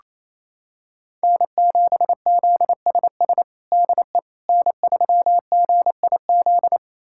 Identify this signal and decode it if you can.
Morse code, heard as a keyed tone, about 27 wpm, 700 Hz. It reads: N7ZSS DE N3GIZ
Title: USA Callsign DE Callsign - 5 Characters - 28wpm